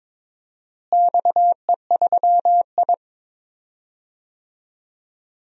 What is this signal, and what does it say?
Morse code, heard as a keyed tone, about 22 wpm, 700 Hz. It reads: XE3I
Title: DXCC-100 Call Signs - 4 Characters - Repeated 2x - 22wpm